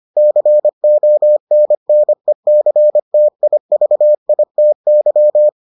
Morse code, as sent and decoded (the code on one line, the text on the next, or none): -.-. --- -. -. -.-. - .. ...- .. - -.--
CONNECTIVITY